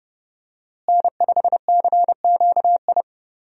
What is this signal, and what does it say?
Morse code, heard as a keyed tone, about 30 wpm, 700 Hz. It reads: N5CQI
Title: USA Call Signs - 5 Characters - Repeated 3x - 30wpm